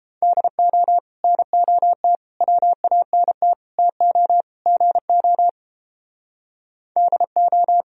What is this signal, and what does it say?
Morse code, heard as a keyed tone, about 31 wpm, 700 Hz. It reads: DO NOT WANT TO GO DO